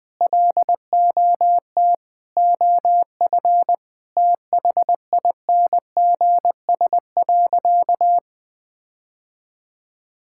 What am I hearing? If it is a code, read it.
Morse code, heard as a keyed tone, about 20 wpm, 700 Hz. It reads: LOT OF THINGS.
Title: Sentences with 4-Word Phrases - 20wpm